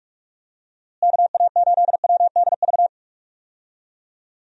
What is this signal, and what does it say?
Morse code, heard as a keyed tone, about 47 wpm, 700 Hz. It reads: KA8WDV
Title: USA Call Signs - 6 Characters - Repeated 3x - 45wpm